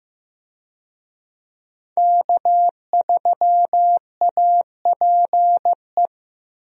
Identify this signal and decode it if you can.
Morse code, heard as a keyed tone, about 15 wpm, 700 Hz. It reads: K3APE
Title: USA Call Signs - 5 Characters - 15wpm